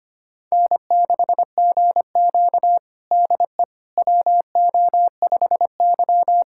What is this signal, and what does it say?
Morse code, heard as a keyed tone, about 25 wpm, 700 Hz. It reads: N6GQ DE WO5Y
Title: USA Callsign DE Callsign - 4 Characters - 25wpm